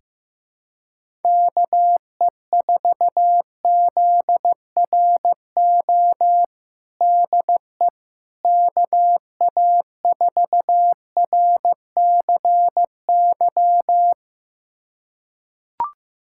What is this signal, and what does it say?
Morse code, heard as a keyed tone, about 15 wpm, 700 Hz. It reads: KE4ZRO DE KA4RCY
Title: USA Callsign DE Callsign 15wpm